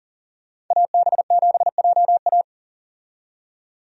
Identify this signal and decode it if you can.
Morse code, heard as a keyed tone, about 40 wpm, 700 Hz. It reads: AB7JA
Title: USA Call Signs - 5 Characters - 40wpm